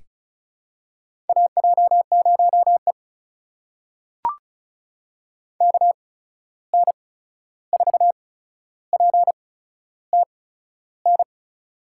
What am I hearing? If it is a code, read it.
Morse code, heard as a keyed tone, about 34 wpm, spaced as for 13 wpm, 700 Hz. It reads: AJ0E KN4PTN
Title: USA Call Signs - 8x Character Spacing - 35wpm